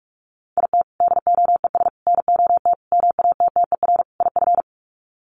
Morse code, heard as a keyed tone, about 44 wpm, 700 Hz. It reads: IT DOES NOT MATTER IF
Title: Sets of 5 Words Encore 45wpm